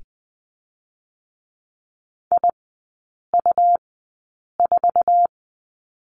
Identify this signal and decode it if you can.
Morse code, heard as a keyed tone, about 20 wpm, spaced as for 8 wpm, 700 Hz. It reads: IU4
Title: 100 Lesson 17e - Letter u - Three Characters 20wpm